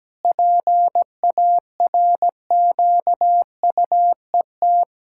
Morse code, heard as a keyed tone, about 17 wpm, 700 Hz. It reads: PARQUET
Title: Pangram Challenge v2 - 17wpm